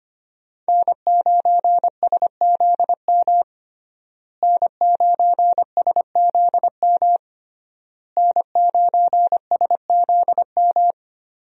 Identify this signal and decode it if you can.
Morse code, heard as a keyed tone, about 25 wpm, 700 Hz. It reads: N9SZM N9SZM N9SZM
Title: USA Call Signs - 5 Characters - Repeated 3x - 25wpm